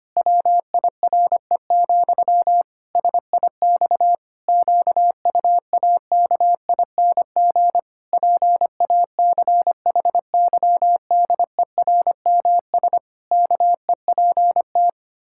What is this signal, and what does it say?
Morse code, heard as a keyed tone, about 25 wpm, 700 Hz. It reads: WIRE, SIX QUAKING PACHYDERMS KEPT
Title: Pangram Challenge v2 - 25wpm